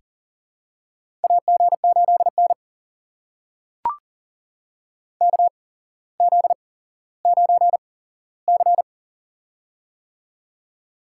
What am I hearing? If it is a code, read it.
Morse code, heard as a keyed tone, about 40 wpm, spaced as for 17 wpm, 700 Hz. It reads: AG8N KZ9C
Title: USA Call Signs - 4 Characters - 8x Character Spacing - 40wpm